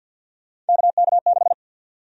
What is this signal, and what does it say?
Morse code, heard as a keyed tone, about 49 wpm, 700 Hz. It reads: KK6
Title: USA Call Sign Prefixes - 50wpm